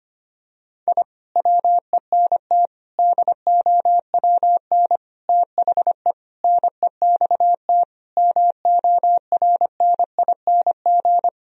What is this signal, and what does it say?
Morse code, heard as a keyed tone, about 25 wpm, 700 Hz. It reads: I WENT DOWN THE NEXT MORNING
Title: Sentences from Top 1000 Words - Review Part 1 - 25wpm